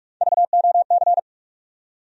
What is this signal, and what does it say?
Morse code, heard as a keyed tone, about 46 wpm, 700 Hz. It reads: VOC
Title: USA Call Sign Postfixes - 3 Letters 45wpm